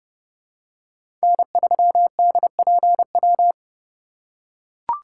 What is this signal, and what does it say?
Morse code, heard as a keyed tone, about 29 wpm, 700 Hz. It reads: N3DPW